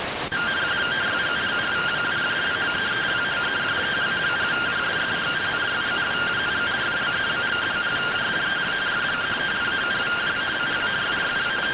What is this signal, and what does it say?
Two-tone teleprinter signal, about 170 Hz shift.
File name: findings / MR AXIAL 71mm 3mm.gen-37 rtty rx